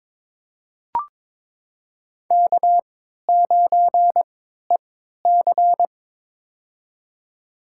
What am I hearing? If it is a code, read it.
Morse code, heard as a keyed tone, about 22 wpm, spaced as for 12 wpm, 700 Hz. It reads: K9EC
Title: USA Call Signs - 4 Characters - 3x Character Spacing - 22wpm